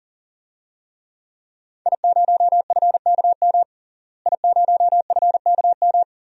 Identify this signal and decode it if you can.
Morse code, heard as a keyed tone, about 40 wpm, 700 Hz. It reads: I0FKM I0FKM
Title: DXCC-100 Call Signs - 5 Characters - Repeated 3x - 40wpm